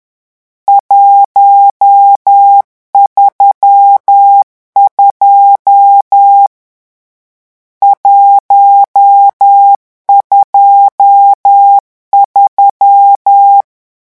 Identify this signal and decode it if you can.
Morse code, heard as a keyed tone, about 10 wpm, 800 Hz. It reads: 132 123
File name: morse mp3 / lecon10-vitesse 10